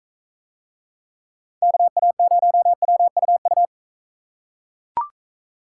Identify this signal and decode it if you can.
Morse code, heard as a keyed tone, about 40 wpm, 700 Hz. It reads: KA0WUU